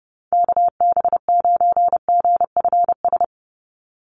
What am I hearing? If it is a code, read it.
Morse code, heard as a keyed tone, about 30 wpm, 700 Hz. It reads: KB9GFS
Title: USA Call Signs 30wpm